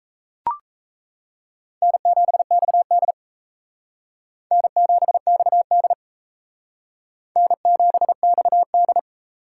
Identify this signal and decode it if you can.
Morse code, heard as a keyed tone, about 35 wpm, 700 Hz. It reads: N7XD N7XD N7XD